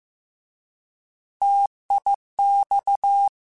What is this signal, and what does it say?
Morse code, heard as a keyed tone, about 15 wpm, 800 Hz. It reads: TIX
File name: Contrasts / Contrasts-mp3s / 3LtrContrasts @13